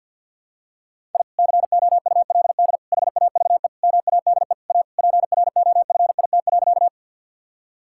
Morse code, heard as a keyed tone, about 52 wpm, 700 Hz. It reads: I COULD HAVE MADE A PROFIT.
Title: Sentences from Top 1000 Words - Review Part 5 - 50wpm